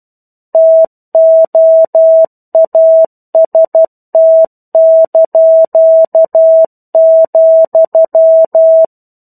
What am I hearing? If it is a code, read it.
Morse code, heard as a keyed tone, about 12 wpm, 650 Hz. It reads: TOAST),
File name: 01-Chapter-01 12 WPM